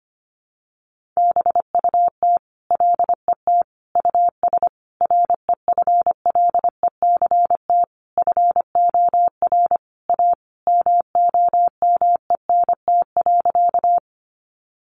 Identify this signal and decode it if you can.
Morse code, heard as a keyed tone, about 25 wpm, 700 Hz. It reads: BUT LET US REFLECT FOR A MOMENT.
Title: Sentences from Top 1500 Words 25wpm